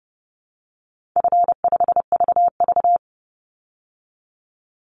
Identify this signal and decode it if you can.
Morse code, heard as a keyed tone, about 30 wpm, 700 Hz. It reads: F5VV
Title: Easy Call Signs Repeated 3x - 30wpm